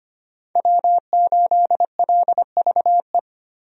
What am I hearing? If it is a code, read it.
Morse code, heard as a keyed tone, about 25 wpm, 700 Hz. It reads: W8LVE
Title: USA Call Signs - 5 Characters - Repeated 3x - 25wpm